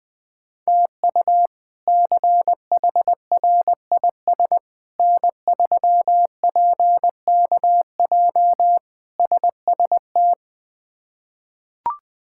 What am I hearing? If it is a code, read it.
Morse code, heard as a keyed tone, about 20 wpm, 700 Hz. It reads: TU CHRIS N3PKJ SST
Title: K1USN SST QSO Practice - 20wpm